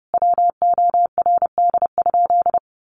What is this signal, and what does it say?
Morse code, heard as a keyed tone, about 30 wpm, 700 Hz. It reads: WORD?
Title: Sentences from Top 1200 Words 30wpm